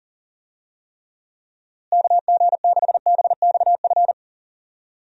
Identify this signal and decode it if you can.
Morse code, heard as a keyed tone, about 40 wpm, 700 Hz. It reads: KG6BXF
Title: USA Call Signs - 6 Characters - Repeated 3x - 40wpm